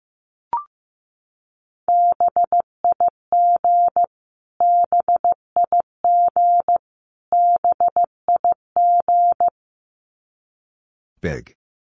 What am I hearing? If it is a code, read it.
Morse code, heard as a keyed tone, about 15 wpm, 700 Hz. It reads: BIG BIG BIG
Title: Top 500 Words Repeated 15wpm